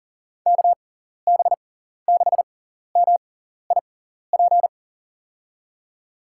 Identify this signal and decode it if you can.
Morse code, heard as a keyed tone, about 37 wpm, spaced as for 14 wpm, 700 Hz. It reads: KB6MIP
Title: USA Call Signs - 6x Character Spacing - 40wpm